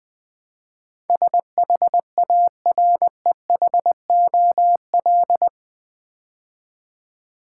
Morse code, heard as a keyed tone, about 20 wpm, 700 Hz. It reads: SHAREHOL
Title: Build A Word - 4 Letters At A Time - 11-Letter Words - 20wpm